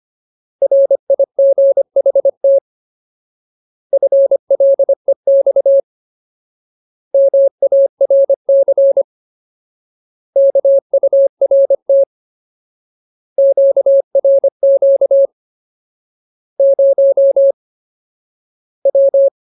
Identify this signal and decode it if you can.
Morse code, heard as a keyed tone, about 25 wpm, 550 Hz. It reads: RIGHT FLEX MARC KURT QRQ 0 W
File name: Ultimate Word List / 25WPM Ultimate